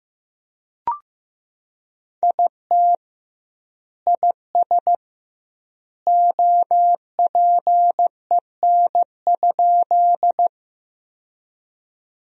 Morse code, heard as a keyed tone, about 15 wpm, 700 Hz. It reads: IT IS OPEN?
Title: 2x Word-Spacing - Sentences from Top 300 Words - 15wpm